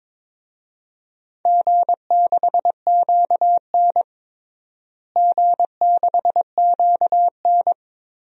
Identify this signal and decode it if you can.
Morse code, heard as a keyed tone, about 22 wpm, 700 Hz. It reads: G6QN G6QN
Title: DXCC-100 Call Signs - 4 Characters - Repeated 3x - 22wpm